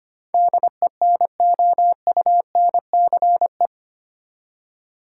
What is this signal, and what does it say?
Morse code, heard as a keyed tone, about 25 wpm, 700 Hz. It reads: DENOUNCE
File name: Eight Letter Words - 25wpm